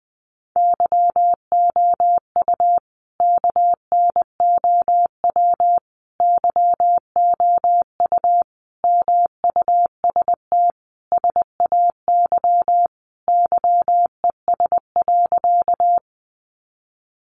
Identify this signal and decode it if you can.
Morse code, heard as a keyed tone, about 20 wpm, 700 Hz. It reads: YOU KNOW YOU MUST SAY YES.
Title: Sentences from Top 500 Words 20wpm